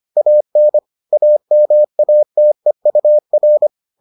Morse code, heard as a keyed tone, about 25 wpm, 600 Hz. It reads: AN AMATEUR